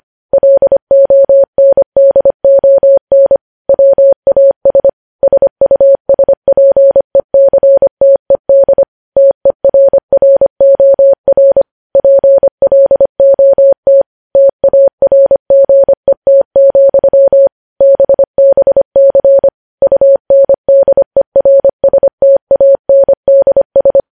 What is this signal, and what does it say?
Morse code, heard as a keyed tone, about 25 wpm, 550 Hz. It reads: LONDON WAS SUSPECTED TERROR PLOT TARGET, BBC UNDERSTANDS
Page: news headlines in morse code at 25 wpm